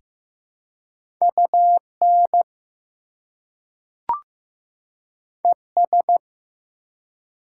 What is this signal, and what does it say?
Morse code, heard as a keyed tone, about 15 wpm, 700 Hz. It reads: UN ES
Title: Prefixes and Suffixes 15wpm